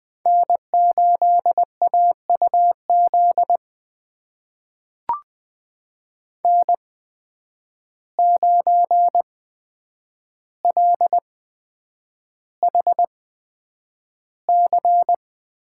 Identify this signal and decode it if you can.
Morse code, heard as a keyed tone, about 20 wpm, spaced as for 9 wpm, 700 Hz. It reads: N8AUZ N9LHC
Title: USA Call Signs - 5 Characters - 8x Character Spacing - 20wpm